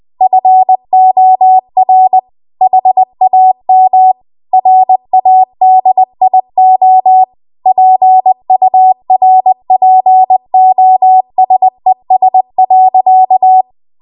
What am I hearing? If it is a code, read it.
Morse code, heard as a keyed tone, about 20 wpm, 750 Hz. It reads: FOR HAM RADIO PURPOSES.